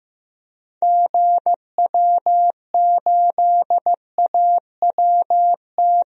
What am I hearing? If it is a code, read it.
Morse code, heard as a keyed tone, about 15 wpm, 700 Hz. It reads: GW8AWT
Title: DXCC-100 Call Signs - 6 Characters - 15wpm